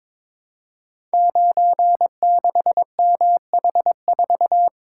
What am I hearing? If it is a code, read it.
Morse code, heard as a keyed tone, about 22 wpm, 690 Hz. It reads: 96MH4
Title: Code Groups - 5 Letter:Numbers - 22wpm